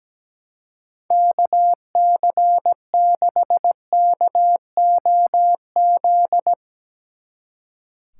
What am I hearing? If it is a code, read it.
Morse code, heard as a keyed tone, about 17 wpm, 700 Hz. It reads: KC6KOZ
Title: Speed Racing - USA Call Signs 17wpm